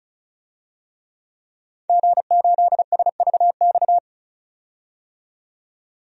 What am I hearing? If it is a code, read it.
Morse code, heard as a keyed tone, about 36 wpm, 700 Hz. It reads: G8SVX
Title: DXCC-100 Call Signs - 5 Characters - 35wpm